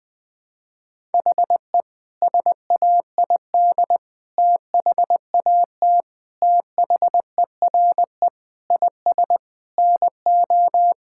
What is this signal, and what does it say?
Morse code, heard as a keyed tone, about 20 wpm, 700 Hz. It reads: HE SAID THAT THERE IS NO
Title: Sentences with 4-Word Phrases - 20wpm